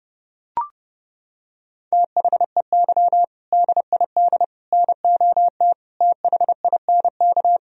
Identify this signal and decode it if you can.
Morse code, heard as a keyed tone, about 30 wpm, 700 Hz. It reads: THEY DID NOT THINK
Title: Sentences from Top 1000 Words - Review Part 3 - 30wpm